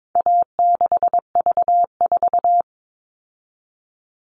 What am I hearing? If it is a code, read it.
Morse code, heard as a keyed tone, about 22 wpm, 700 Hz. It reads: A6V4
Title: All Cut Numbers - 4-Digit 22wpm